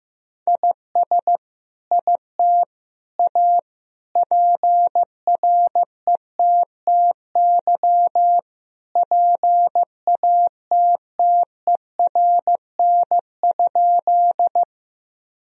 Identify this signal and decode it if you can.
Morse code, heard as a keyed tone, about 15 wpm, 700 Hz. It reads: IS IT A PRETTY PATTERN?
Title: Sentences from Top 1000 Words - Review Part 2 - 15wpm